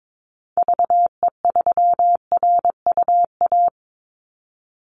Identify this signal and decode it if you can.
Morse code, heard as a keyed tone, about 22 wpm, 700 Hz. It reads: VE3RUA
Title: Call Signs - 22wpm